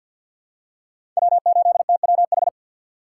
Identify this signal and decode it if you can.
Morse code, heard as a keyed tone, about 46 wpm, 700 Hz. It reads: W8TWH
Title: USA Call Signs - 2x Character Spacing - 50wpm